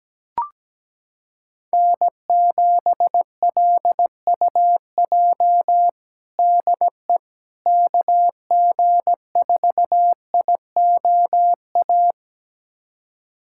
Morse code, heard as a keyed tone, about 17 wpm, 700 Hz. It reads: N7LUJ DE KG4IOA
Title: USA Callsign DE Callsign 17wpm